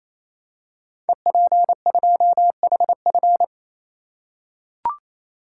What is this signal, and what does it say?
Morse code, heard as a keyed tone, about 29 wpm, 700 Hz. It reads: EP2HF
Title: Code Groups - 5 Letter:Numbers - 28wpm